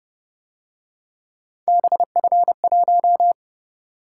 Morse code, heard as a keyed tone, about 28 wpm, 700 Hz. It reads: BF1